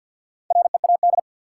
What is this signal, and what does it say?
Morse code, heard as a keyed tone, about 52 wpm, 700 Hz. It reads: READ
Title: Sentences from Top 200 Words - New Word Drill - 50wpm